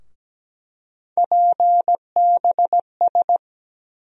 Morse code, heard as a keyed tone, about 17 wpm, 700 Hz. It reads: PBS
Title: USA Call Sign Postfixes - 3 Letters 17wpm